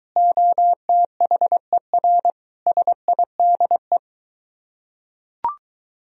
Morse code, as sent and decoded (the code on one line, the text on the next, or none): --- - .... .-. / ... .. -.. .
OTHER SIDE